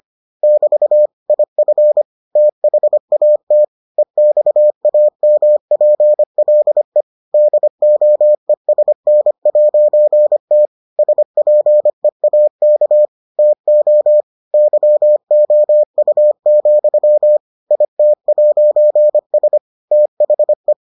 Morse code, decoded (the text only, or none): <BT> IF THAT EXAMPLE DOESN'T SPEAK TO YOU, IT'S THE